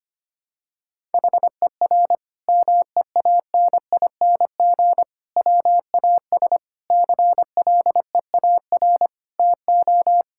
Morse code, heard as a keyed tone, about 25 wpm, 700 Hz. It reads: HER MEANING WAS CLEAR TO
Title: Sentences from Top 1000 Words - Review Part 6 - 25wpm